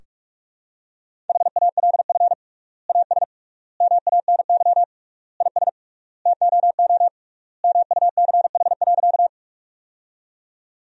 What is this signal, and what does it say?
Morse code, heard as a keyed tone, about 46 wpm, 700 Hz. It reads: HALF AS MANY IS TOO MUCH.
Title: 3x Word-Spacing - Sentences from Top 400 Words - 45wpm